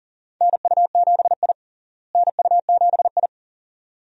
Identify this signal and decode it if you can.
Morse code, heard as a keyed tone, about 40 wpm, 700 Hz. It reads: NU7I NU7I
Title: USA Call Signs - 4 Characters - Repeated 2x - 40wpm